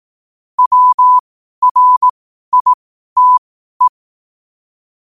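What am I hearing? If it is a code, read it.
Morse code, heard as a keyed tone, about 18 wpm, spaced as for 12 wpm, 1000 Hz. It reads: WRITE
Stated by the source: Morse code quotes at 12 words per minute.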